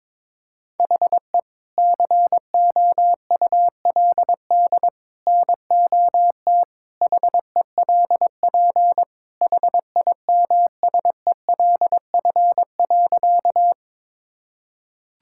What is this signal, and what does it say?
Morse code, heard as a keyed tone, about 21 wpm, 700 Hz. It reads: HE COULD NOT HELP HIMSELF.
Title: Sentences from Top 1600 Words 22wpm